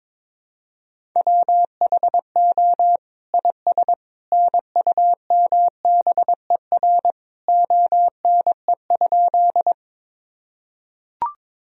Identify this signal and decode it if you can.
Morse code, heard as a keyed tone, about 22 wpm, 700 Hz. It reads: WHO IS NUMBER ONE?
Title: Sentences from Top 1000 Words - Review Part 1 - 22wpm